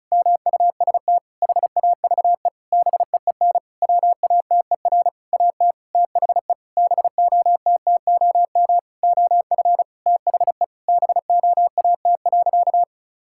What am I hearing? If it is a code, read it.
Morse code, heard as a keyed tone, about 34 wpm, 700 Hz. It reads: MUST HAVE BEEN WATER AT THE BOTTOM OF THE BOAT.
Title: Sentences from Top 1000 Words - Review Part 6 - 35wpm